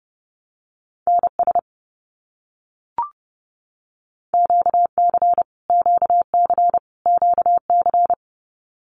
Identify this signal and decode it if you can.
Morse code, heard as a keyed tone, about 30 wpm, 700 Hz. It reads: NS QC QC QC
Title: Canadian Provinces and Territories - Repeated 3x - 30wpm